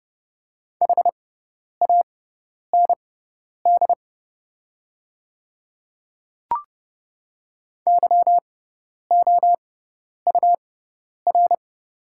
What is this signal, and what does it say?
Morse code, heard as a keyed tone, about 30 wpm, spaced as for 10 wpm, 700 Hz. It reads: HAND YOUR
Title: 6x Character Spacing - Top 300 Words - 30wpm